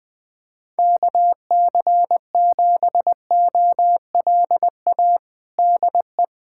Morse code, heard as a keyed tone, about 21 wpm, 700 Hz. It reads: KC7OLA DE
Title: USA Callsign DE Callsign 20wpm